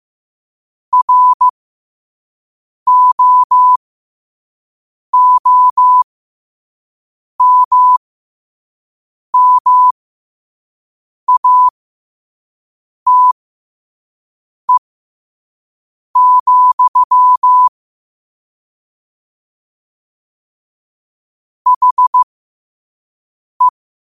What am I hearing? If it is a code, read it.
Morse code, heard as a keyed tone, about 15 wpm, spaced as for 5 wpm, 1000 Hz. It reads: ROOMMATE, HE
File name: qotdfiles / 5wpm